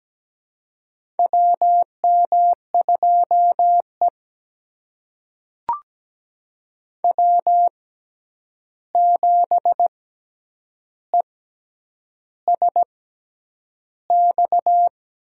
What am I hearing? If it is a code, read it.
Morse code, heard as a keyed tone, about 17 wpm, spaced as for 8 wpm, 700 Hz. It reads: WM2E W7ESX
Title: USA Call Signs - 6x Character Spacing - 17wpm